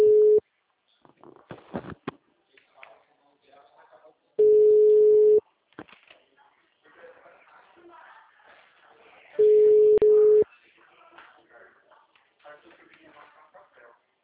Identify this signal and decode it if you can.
Ringback tone (Europe, 425 Hz), with speech.